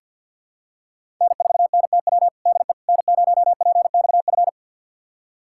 Morse code, heard as a keyed tone, about 47 wpm, 700 Hz. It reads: N4NTW DE N0PXF
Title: USA Callsign DE Callsign - 5 Characters - 50wpm